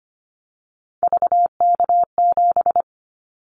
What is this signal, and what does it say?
Morse code, heard as a keyed tone, about 25 wpm, 700 Hz. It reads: VK7
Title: Callsign Prefixes for Top 100 DXCC - 25wpm